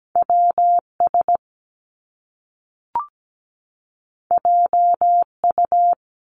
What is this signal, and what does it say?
Morse code, heard as a keyed tone, about 17 wpm, 700 Hz. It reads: WS JU
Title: Top Two Letter Combinations in Words 17wpm